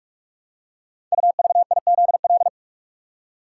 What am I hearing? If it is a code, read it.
Morse code, heard as a keyed tone, about 44 wpm, 700 Hz. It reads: UVIZL
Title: Code Groups - 5 Letters - 45wpm